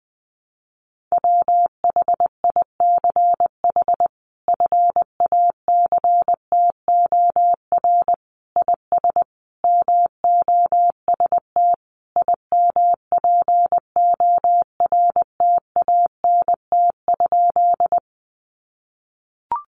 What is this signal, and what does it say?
Morse code, heard as a keyed tone, about 20 wpm, 700 Hz. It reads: WHICH FACTOR IS MOST IMPORTANT?
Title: Sentences from Top 1300 Words 20wpm